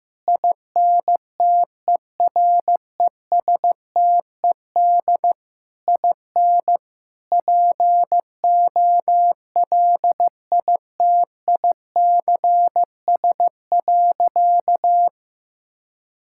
Morse code, heard as a keyed tone, about 15 wpm, 700 Hz. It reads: INTERESTED IN POLITICS.
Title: Sentences from Top 1000 Words - Review Part 6 - 15wpm